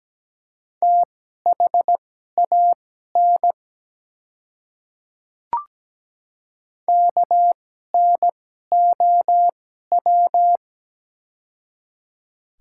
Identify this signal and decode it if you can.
Morse code, heard as a keyed tone, about 17 wpm, spaced as for 12 wpm, 690 Hz. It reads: THAN KNOW
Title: 2x Character Spacing - Top 100 Words - 17wpm